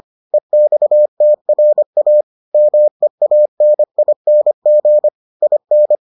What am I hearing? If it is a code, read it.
Morse code, heard as a keyed tone, about 25 wpm, 600 Hz. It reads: EXTRA MEANING IN